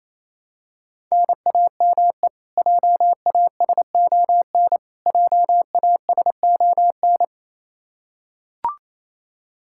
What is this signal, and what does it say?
Morse code, heard as a keyed tone, about 28 wpm, 700 Hz. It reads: NAME JASON JASON
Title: Practice QSO - Exchange 1 - Beginning - 28wpm